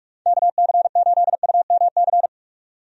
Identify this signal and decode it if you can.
Morse code, heard as a keyed tone, about 46 wpm, 700 Hz. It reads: KC8UMC
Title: USA Call Signs - 6 Characters - 45wpm